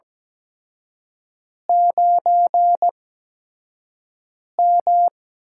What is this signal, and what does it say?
Morse code, heard as a keyed tone, about 17 wpm, spaced as for 5 wpm, 700 Hz. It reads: 9M
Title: USA Call Signs - 8x Character Spacing - 17wpm